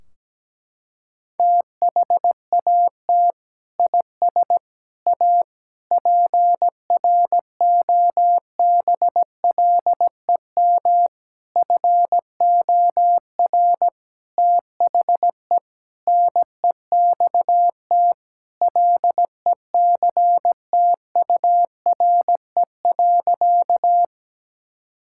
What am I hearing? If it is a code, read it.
Morse code, heard as a keyed tone, about 17 wpm, 700 Hz. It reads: THAT IS A PROBLEM FOR THE NEXT LECTURE.
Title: Sentences from Top 1600 Words 17wpm